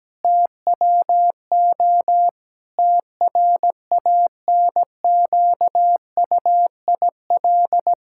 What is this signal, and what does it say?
Morse code, heard as a keyed tone, about 17 wpm, 700 Hz. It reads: TWO TRANQUIL
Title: Pangram Challenge v2 - 17wpm